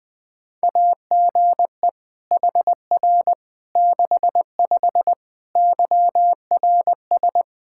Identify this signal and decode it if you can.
Morse code, heard as a keyed tone, about 21 wpm, 700 Hz. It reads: AGE HR 65 YRS
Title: Practice QSO - Exchange 3 - More Info - 20wpm